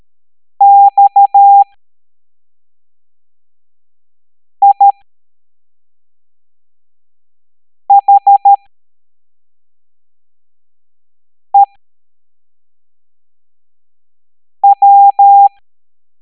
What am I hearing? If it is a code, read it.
Morse code, heard as a keyed tone, about 13 wpm, spaced as for 3 wpm, 800 Hz. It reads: XIHEW